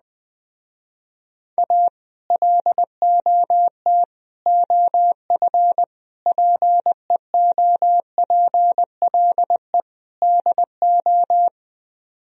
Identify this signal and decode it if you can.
Morse code, heard as a keyed tone, about 20 wpm, 700 Hz. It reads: A LOT OF PEOPLE DO
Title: Speed-Racing - Sets of 5 Words 17wpm